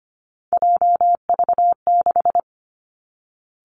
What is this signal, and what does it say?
Morse code, heard as a keyed tone, about 25 wpm, 700 Hz. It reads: JV6
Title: Three Challenging Letter-Number Combinations 25wpm